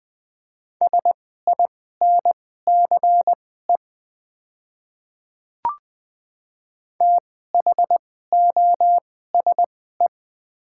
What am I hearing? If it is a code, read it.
Morse code, heard as a keyed tone, about 20 wpm, spaced as for 15 wpm, 690 Hz. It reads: SINCE THOSE
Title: 2x Character Spacing - Top 300 Words - 20wpm